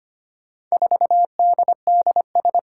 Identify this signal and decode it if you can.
Morse code, heard as a keyed tone, about 25 wpm, 700 Hz. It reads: 4DDS
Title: Four Challenging Letter-Number Combinations 25wpm